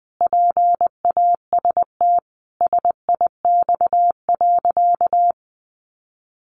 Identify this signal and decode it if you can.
Morse code, heard as a keyed tone, about 20 wpm, 700 Hz. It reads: PAST SIX.
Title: Sentences from Top 600 Words 20wpm